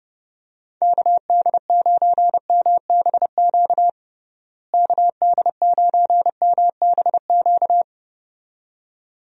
Morse code, heard as a keyed tone, about 30 wpm, 700 Hz. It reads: KD9MBQ KD9MBQ
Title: USA Call Signs - 6 Characters - Repeated 2x - 30wpm